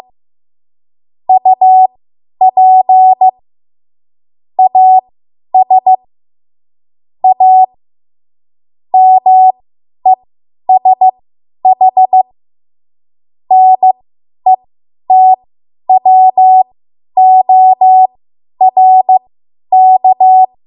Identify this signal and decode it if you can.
Morse code, heard as a keyed tone, about 15 wpm, spaced as for 10 wpm, 750 Hz. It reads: UP AS A MESH NETWORK
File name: Morse / 10WPM